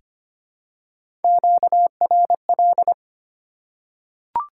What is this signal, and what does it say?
Morse code, heard as a keyed tone, about 25 wpm, 700 Hz. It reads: QRL
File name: Top 90 QSO Elements - 2x Character Spacing - 25wpm